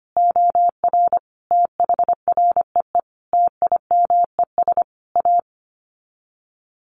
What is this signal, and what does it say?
Morse code, heard as a keyed tone, about 25 wpm, 700 Hz. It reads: OR THREE TIMES A
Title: Sets of 5 Words Encore2 25wpm